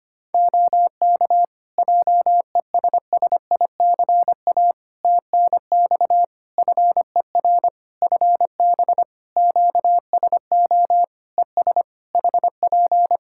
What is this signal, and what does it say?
Morse code, heard as a keyed tone, about 25 wpm, 700 Hz. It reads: OK JESSICA TNX FER FB QSO ES HP
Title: Practice QSO - Exchange 4 - Ending - 25wpm